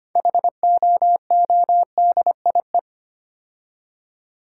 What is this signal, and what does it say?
Morse code, heard as a keyed tone, about 25 wpm, 700 Hz. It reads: HOODIE
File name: practice / 25wpm / Difficult Words with Hints - 25wpm